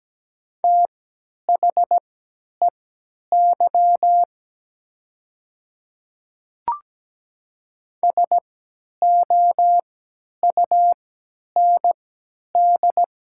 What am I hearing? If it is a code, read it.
Morse code, heard as a keyed tone, about 17 wpm, spaced as for 10 wpm, 700 Hz. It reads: THEY SOUND
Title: 3x Character Spacing - Top 100 Words - 17wpm